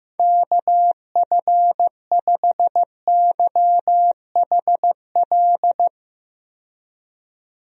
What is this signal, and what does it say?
Morse code, heard as a keyed tone, about 15 wpm, 700 Hz. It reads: KF5YHL
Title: USA Call Signs - 6 Characters - 15wpm